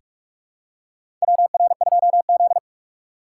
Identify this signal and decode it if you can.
Morse code, heard as a keyed tone, about 43 wpm, 700 Hz. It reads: WR2Z
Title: USA Call Signs - 4 Characters - 2x Character Spacing - 45wpm